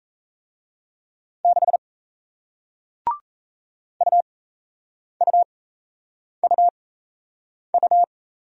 Morse code, heard as a keyed tone, about 38 wpm, spaced as for 8 wpm, 700 Hz. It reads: 6 UUUU